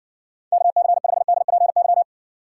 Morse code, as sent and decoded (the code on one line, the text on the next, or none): -.- -.-. ..... -.. .--. -.--
KC5DPY